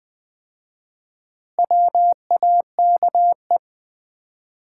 Morse code, heard as a keyed tone, about 20 wpm, 700 Hz. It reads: WAKE
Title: 4x Word-Spacing - Sets of 5 Words - 20wpm